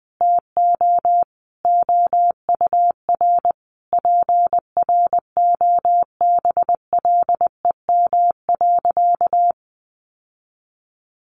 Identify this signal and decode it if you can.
Morse code, heard as a keyed tone, about 20 wpm, 700 Hz. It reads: TO OUR PROBLEM.
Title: Sentences from Top 1300 Words 20wpm